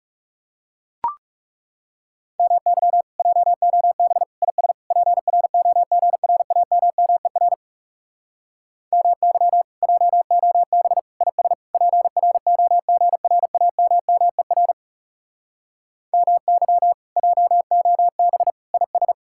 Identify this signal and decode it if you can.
Morse code, heard as a keyed tone, about 40 wpm, 700 Hz. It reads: MY JOB IS PROGRAMMER MY JOB IS PROGRAMMER MY JOB IS